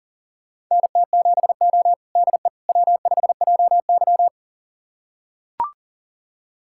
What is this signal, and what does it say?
Morse code, heard as a keyed tone, about 40 wpm, 700 Hz. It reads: NT7O DE W5JY
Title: USA Callsign DE Callsign - 4 Characters - 40wpm